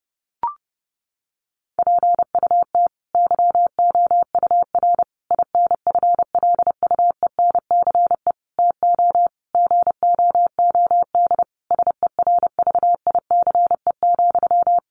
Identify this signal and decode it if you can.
Morse code, heard as a keyed tone, about 30 wpm, 700 Hz. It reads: PUT YOUR INFLUENCE TO GOOD SERVICE,
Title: Sentences from Top 1100 Words 30wpm